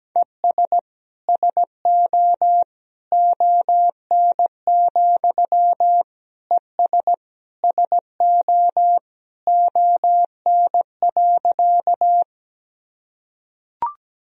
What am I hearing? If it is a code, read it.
Morse code, heard as a keyed tone, about 17 wpm, 700 Hz. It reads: ES SO ON, ES SO ON.
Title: Sentences from Top 100 Words and Abbreviations 17wpm